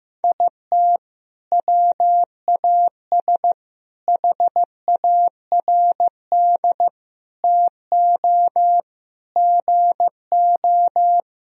Morse code, heard as a keyed tone, about 15 wpm, 700 Hz. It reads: IT WAS HARD TO GO
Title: Sentences from Top 1000 Words - Review Part 1 - 15wpm